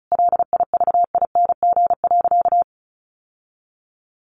Morse code, heard as a keyed tone, about 34 wpm, 700 Hz. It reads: LIVING.
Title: Sentences from Top 1100 Words 35wpm